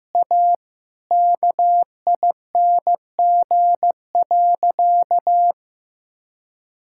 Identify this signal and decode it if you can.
Morse code, heard as a keyed tone, about 16 wpm, 700 Hz. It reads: A KING.
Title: Sentences from Top 1000 Words - Review Part 5 - 15wpm